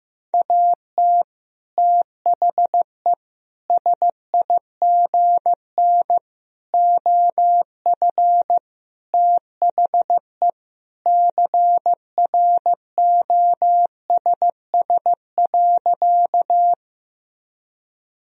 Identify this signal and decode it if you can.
Morse code, heard as a keyed tone, about 15 wpm, 700 Hz. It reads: AT THE SIGN OF THE CROSS.
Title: Sentences from Top 1000 Words - Review Part 1 - 15wpm